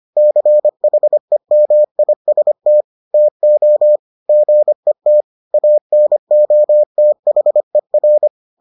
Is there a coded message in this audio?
Yes: Morse code from 0.2 to 8.3 s, heard as a keyed tone, about 25 wpm, 600 Hz. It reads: CHEMIST TO GET ANOTHER